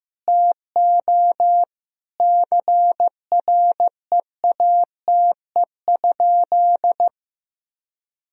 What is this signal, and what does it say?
Morse code, heard as a keyed tone, about 15 wpm, 700 Hz. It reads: TO CREATE?
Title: Sentences from Top 1000 Words - Review Part 3 - 15wpm